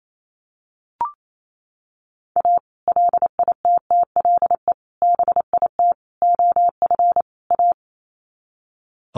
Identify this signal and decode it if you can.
Morse code, heard as a keyed tone, about 28 wpm, 700 Hz. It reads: A LITTLE BIT OF A
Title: Sets of 5 Words 28wpm